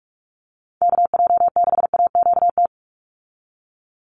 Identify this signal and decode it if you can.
Morse code, heard as a keyed tone, about 44 wpm, 700 Hz. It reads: KJ6AQT
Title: USA Call Signs 45wpm